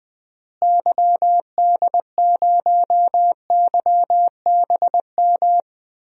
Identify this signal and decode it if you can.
Morse code, heard as a keyed tone, about 20 wpm, 700 Hz. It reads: YD0YBM
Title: DXCC-100 Call Signs - 6 Characters - Repeated 3x - 20wpm